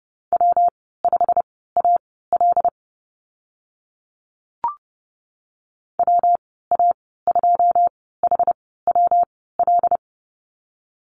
Morse code, heard as a keyed tone, about 30 wpm, spaced as for 17 wpm, 700 Hz. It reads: W5AL WA2HWL
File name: Lesson 19g - Letter w - Call Signs 30wpm